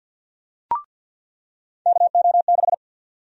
Morse code, heard as a keyed tone, about 48 wpm, 700 Hz. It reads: KO6
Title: USA Call Sign Prefixes - 50wpm